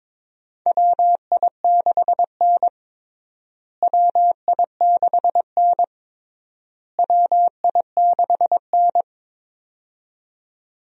Morse code, heard as a keyed tone, about 22 wpm, 700 Hz. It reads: WI6N WI6N WI6N
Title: USA Call Signs - 4 Characters - Repeated 3x - 22wpm